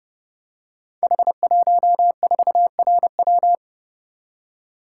Morse code, heard as a keyed tone, about 30 wpm, 700 Hz. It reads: H14RW
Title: Code Groups - 5 Letter:Numbers - 30wpm